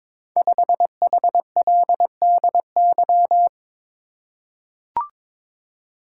Morse code, heard as a keyed tone, about 22 wpm, 700 Hz. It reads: 5HLDY
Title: Code Groups - 5 Letter:Numbers - 22wpm